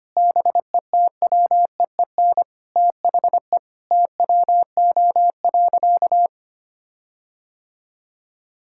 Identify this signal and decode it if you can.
Morse code, heard as a keyed tone, about 25 wpm, 700 Hz. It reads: BETWEEN THE TWO.
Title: Sentences from Top 1600 Words 25wpm